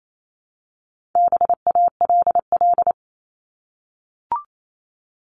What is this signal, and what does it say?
Morse code, heard as a keyed tone, about 28 wpm, 700 Hz. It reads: BALL
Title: Top 751-1000 Words 28wpm